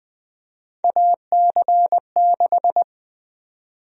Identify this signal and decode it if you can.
Morse code, heard as a keyed tone, about 20 wpm, 700 Hz. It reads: AC6
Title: USA Call Sign Prefixes - 20wpm